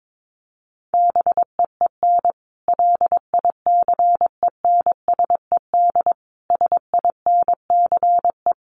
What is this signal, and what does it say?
Morse code, heard as a keyed tone, about 22 wpm, 700 Hz. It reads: BEEN LICENSED SINCE
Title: Common QSO Phrases 22wpm